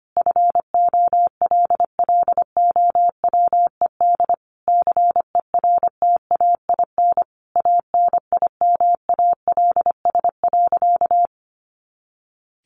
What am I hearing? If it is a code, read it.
Morse code, heard as a keyed tone, about 25 wpm, 700 Hz. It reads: FOLLOWED CERTAIN ANIMALS.
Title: A Sound Of Thunder 25wpm